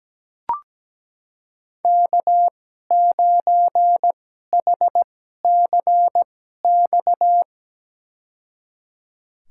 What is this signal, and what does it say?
Morse code, heard as a keyed tone, about 17 wpm, spaced as for 12 wpm, 700 Hz. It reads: K9HCX
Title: USA Call Signs - 5 Characters - 2x Character Spacing - 17wpm